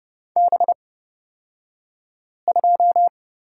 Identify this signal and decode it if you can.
Morse code, heard as a keyed tone, about 31 wpm, spaced as for 5 wpm, 700 Hz. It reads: B2
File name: Lesson 36d - BT - Two Characters 30wpm